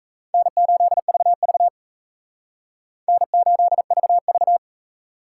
Morse code, heard as a keyed tone, about 37 wpm, 700 Hz. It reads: N8VV N8VV